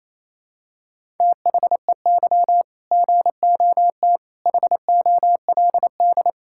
Morse code, heard as a keyed tone, about 28 wpm, 700 Hz. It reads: THEY GOT HOLD
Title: Sentences from Top 1600 Words 28wpm